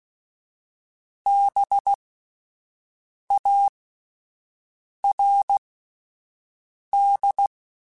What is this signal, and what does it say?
Morse code, heard as a keyed tone, about 16 wpm, spaced as for 5 wpm, 800 Hz. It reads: BARD
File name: Spaced Alphas / Spaced Alphas mp3s / Bs sp @10